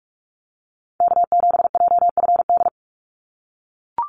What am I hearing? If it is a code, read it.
Morse code, heard as a keyed tone, about 44 wpm, 700 Hz. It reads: K7JFD
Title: NA SOTA Activator Callsigns 45wpm